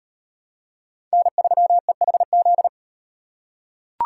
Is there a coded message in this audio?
Yes: Morse code from 1.1 to 2.7 s, heard as a keyed tone, about 38 wpm, 700 Hz. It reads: N3EHZ